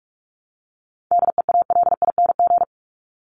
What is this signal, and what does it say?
Morse code, heard as a keyed tone, about 44 wpm, 700 Hz. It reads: DEALING
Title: ING Suffix - Top 100 Words with Base Words 45wpm